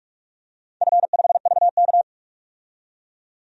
Morse code, heard as a keyed tone, about 47 wpm, 700 Hz. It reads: F5VK
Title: DXCC-100 Call Signs - 4 Characters - 45wpm